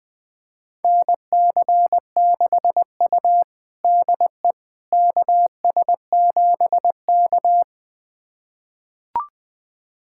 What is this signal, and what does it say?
Morse code, heard as a keyed tone, about 21 wpm, 700 Hz. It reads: NC6U DE KS7K
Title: USA Callsign DE Callsign - 4 Characters - 20wpm